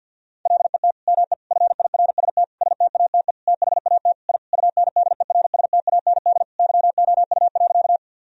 Morse code, heard as a keyed tone, about 49 wpm, 700 Hz. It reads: LET ME FIRST STATE THAT I UNDERSTAND YOU.
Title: Sentences from Top 1000 Words - Review Part 2 - 50wpm